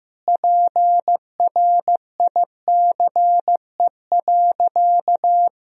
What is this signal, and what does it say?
Morse code, heard as a keyed tone, about 15 wpm, 700 Hz. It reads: PRICE.
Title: Sentences from Top 1000 Words - Review Part 4 - 15wpm